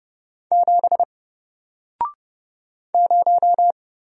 Morse code, heard as a keyed tone, about 31 wpm, 700 Hz. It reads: 7 0
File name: Single-Letter-Number-30wpm